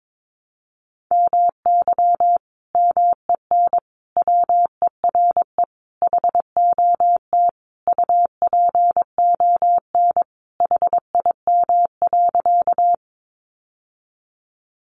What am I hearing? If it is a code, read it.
Morse code, heard as a keyed tone, about 22 wpm, 700 Hz. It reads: MY MEN WERE HOT UPON HIM.
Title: Sentences from Top 1100 Words 22wpm